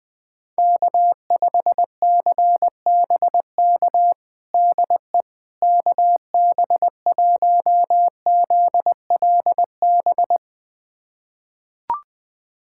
Morse code, heard as a keyed tone, about 20 wpm, 690 Hz. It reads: K5CBK DE KB1ZLB
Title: USA Callsign DE Callsign 20wpm